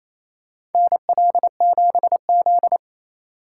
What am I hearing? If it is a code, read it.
Morse code, heard as a keyed tone, about 28 wpm, 700 Hz. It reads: NL7Z
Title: USA Call Signs - 4 Characters - 2x Character Spacing - 28wpm